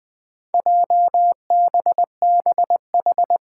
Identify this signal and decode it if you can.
Morse code, heard as a keyed tone, about 20 wpm, 700 Hz. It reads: JBBH
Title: Four Challenging Letter-Number Combinations 20wpm